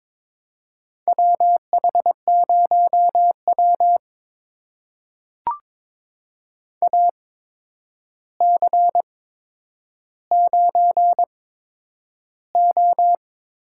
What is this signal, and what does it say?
Morse code, heard as a keyed tone, about 22 wpm, spaced as for 9 wpm, 700 Hz. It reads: WH0W AC9O
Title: USA Call Signs - 4 Characters - 8x Character Spacing - 22wpm